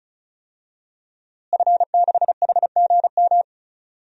Morse code, heard as a keyed tone, about 36 wpm, 700 Hz. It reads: F6HGM
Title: DXCC-100 Call Signs - 5 Characters - Repeated 2x - 35wpm